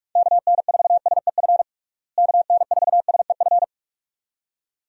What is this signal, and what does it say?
Morse code, heard as a keyed tone, about 46 wpm, 700 Hz. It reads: KN4SEF KN4SEF
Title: USA Call Signs - 6 Characters - Repeated 2x - 45wpm